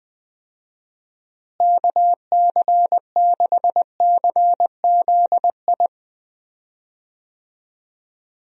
Morse code, heard as a keyed tone, about 20 wpm, 700 Hz. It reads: KC6CZI
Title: USA Call Signs - 6 Characters - Repeated 2x - 20wpm